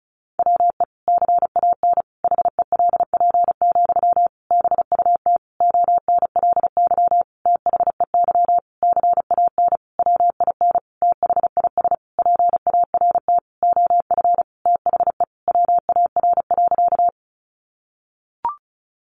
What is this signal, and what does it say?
Morse code, heard as a keyed tone, about 34 wpm, 700 Hz. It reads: WE CAN HELP, BUT ONLY THEY CAN WIN THIS PART OF THE WAR.
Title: Sentences from Top 1500 Words 35wpm